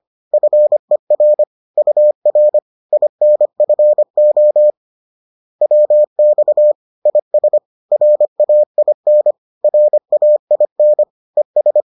Morse code, heard as a keyed tone, about 25 wpm, 600 Hz. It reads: FER UR INFO WX IS RAIN RAIN ES